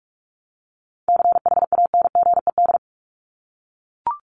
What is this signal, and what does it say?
Morse code, heard as a keyed tone, about 44 wpm, 700 Hz. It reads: CHANGED
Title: ED Suffix - Top 100 Words with Base Words 45wpm